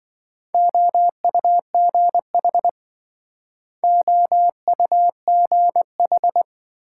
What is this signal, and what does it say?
Morse code, heard as a keyed tone, about 26 wpm, 700 Hz. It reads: OUGH OUGH